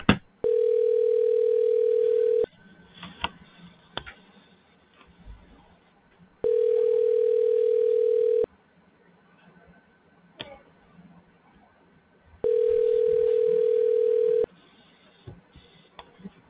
Ringback tone (North America), 440 + 480 Hz.